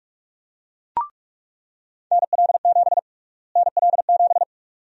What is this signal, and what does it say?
Morse code, heard as a keyed tone, about 46 wpm, 700 Hz. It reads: NL7 NL7
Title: USA Call Sign Prefixes - Repeated 2x - 45wpm